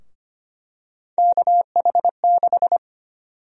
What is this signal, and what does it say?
Morse code, heard as a keyed tone, about 25 wpm, 700 Hz. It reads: KH6
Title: USA Call Sign Prefixes - Repeated 2x - 25wpm